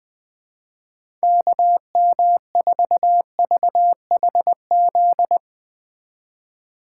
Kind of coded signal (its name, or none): Morse code